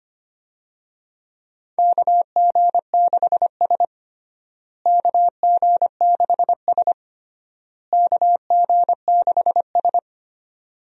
Morse code, heard as a keyed tone, about 25 wpm, 700 Hz. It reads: KG6S KG6S KG6S
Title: USA Call Signs - 4 Characters - Repeated 3x - 25wpm